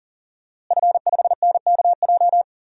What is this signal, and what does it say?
Morse code, heard as a keyed tone, about 40 wpm, 700 Hz. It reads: F5NKJ
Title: DXCC-100 Call Signs - 5 Characters - Repeated 3x - 40wpm